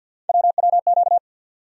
Morse code, heard as a keyed tone, about 51 wpm, 700 Hz. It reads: WWQ
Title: USA Call Sign Postfixes - 3 Letters 50wpm